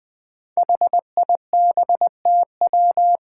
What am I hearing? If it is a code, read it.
Morse code, heard as a keyed tone, about 21 wpm, 700 Hz. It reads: HIBTW
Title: Five Challenging Letter-Number Combinations 20wpm